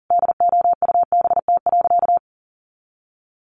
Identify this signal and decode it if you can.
Morse code, heard as a keyed tone, about 40 wpm, 700 Hz. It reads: DOUBT.
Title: Sentences from Top 1500 Words 40wpm